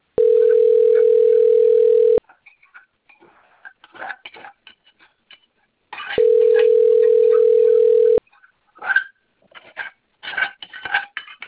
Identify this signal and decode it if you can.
Ringback tone (North America), 440 + 480 Hz.